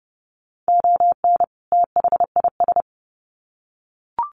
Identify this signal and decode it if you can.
Morse code, heard as a keyed tone, about 30 wpm, 700 Hz. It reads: ON THIS
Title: Sets of 2 Words 30wpm